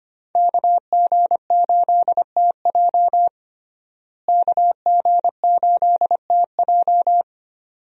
Morse code, heard as a keyed tone, about 25 wpm, 700 Hz. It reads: KG8TJ KG8TJ
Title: USA Call Signs - 5 Characters - Repeated 2x - 25wpm